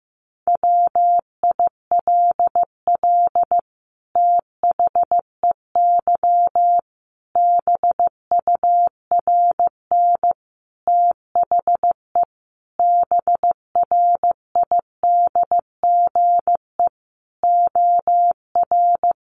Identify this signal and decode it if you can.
Morse code, heard as a keyed tone, about 15 wpm, 700 Hz. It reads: WILL THEY BURN THE BRIDGE OR
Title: Sentences from Top 1300 Words 15wpm